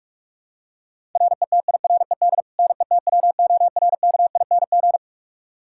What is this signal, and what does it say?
Morse code, heard as a keyed tone, about 41 wpm, 700 Hz. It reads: RETIRED NETWORKING
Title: K1USN SST QSO Practice 45wpm